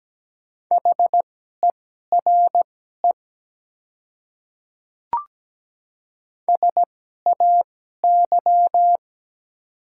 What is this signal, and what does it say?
Morse code, heard as a keyed tone, about 17 wpm, spaced as for 12 wpm, 700 Hz. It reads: HERE SAY
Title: 2x Character Spacing - Top 200 Words - 17wpm